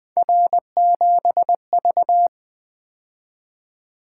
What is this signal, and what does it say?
Morse code, heard as a keyed tone, about 20 wpm, 700 Hz. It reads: R7V
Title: Three Letter-Number Combinations - 20wpm